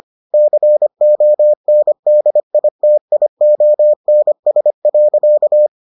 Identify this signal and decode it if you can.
Morse code, heard as a keyed tone, about 25 wpm, 600 Hz. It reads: CONDITIONS.